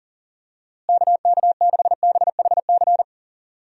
Morse code, heard as a keyed tone, about 40 wpm, 700 Hz. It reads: KK6BHC
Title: USA Call Signs - 6 Characters - Repeated 2x - 40wpm